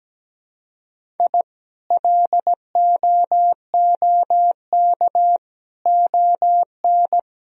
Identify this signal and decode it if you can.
Morse code, heard as a keyed tone, about 17 wpm, 700 Hz. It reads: I LOOK ON
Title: Sentences from Top 1000 Words - Review Part 1 - 17wpm